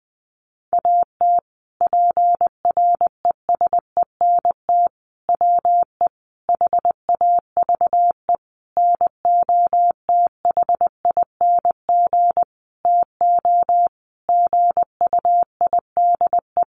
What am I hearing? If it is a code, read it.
Morse code, heard as a keyed tone, about 20 wpm, 700 Hz. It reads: AT PRESENT WE HAVE NOTHING TO GUIDE
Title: Sentences from Top 800 Words 20wpm